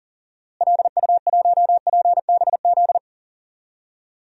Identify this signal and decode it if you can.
Morse code, heard as a keyed tone, about 40 wpm, 700 Hz. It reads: LU1PBZ
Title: DXCC-100 Call Signs - 6 Characters - Repeated 2x - 40wpm